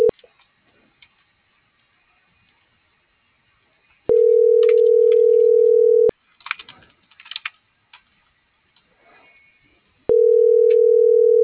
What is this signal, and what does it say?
Ringback tone (North America), 440 + 480 Hz.